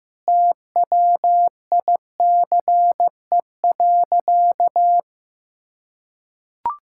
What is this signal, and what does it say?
Morse code, heard as a keyed tone, about 15 wpm, 700 Hz. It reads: TWICE.
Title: Sentences from Top 1600 Words 15wpm